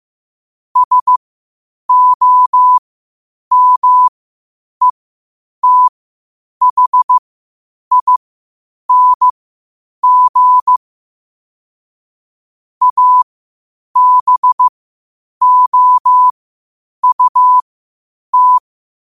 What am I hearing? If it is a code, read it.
Morse code, heard as a keyed tone, about 15 wpm, spaced as for 8 wpm, 1000 Hz. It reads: SOMETHING ABOUT
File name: qotdfiles / 8wpm